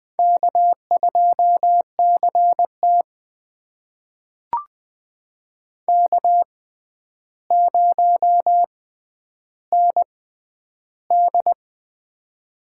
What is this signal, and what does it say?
Morse code, heard as a keyed tone, about 20 wpm, spaced as for 10 wpm, 700 Hz. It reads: K2CT K0ND
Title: USA Call Signs - 4 Characters - 6x Character Spacing - 20wpm